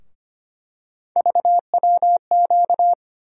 Morse code, heard as a keyed tone, about 25 wpm, 700 Hz. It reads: VWQ
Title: USA Call Sign Postfixes - 3 Letters 25wpm